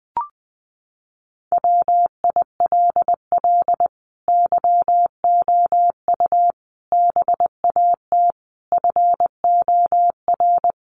Morse code, heard as a keyed tone, about 20 wpm, 700 Hz. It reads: WILL YOU BAT FOR
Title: Sentences from Top 1500 Words 20wpm